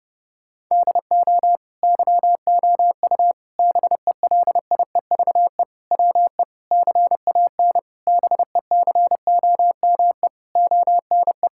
Morse code, heard as a keyed tone, about 30 wpm, 700 Hz. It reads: DO YOU BELIEVE WE CAN BECOME ONE
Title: Sentences from Top 1000 Words - Review Part 6 - 30wpm